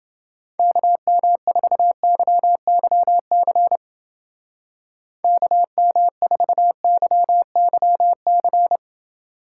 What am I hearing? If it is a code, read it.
Morse code, heard as a keyed tone, about 29 wpm, 700 Hz. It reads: KM4YYC KM4YYC